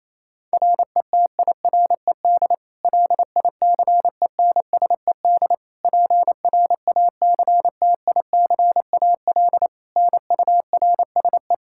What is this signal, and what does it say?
Morse code, heard as a keyed tone, about 28 wpm, 700 Hz. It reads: RETIRED LICENSED PRACTICAL NURSE
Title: K1USN SST QSO Practice 28wpm